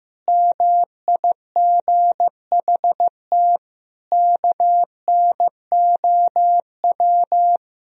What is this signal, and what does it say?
Morse code, heard as a keyed tone, about 15 wpm, 700 Hz. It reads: MIGHT KNOW
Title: Sentences from Top 500 Words and Abbreviations 15wpm